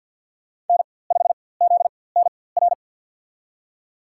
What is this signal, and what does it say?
Morse code, heard as a keyed tone, about 51 wpm, spaced as for 23 wpm, 700 Hz. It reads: N5ZNR
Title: USA Call Signs - 5 Characters - 4x Character Spacing - 50wpm